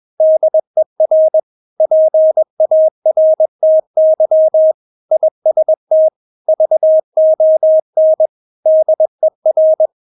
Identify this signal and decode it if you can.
Morse code, heard as a keyed tone, about 21 wpm, 630 Hz. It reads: DER PARTY IST VON DER
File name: Witz2 21wpm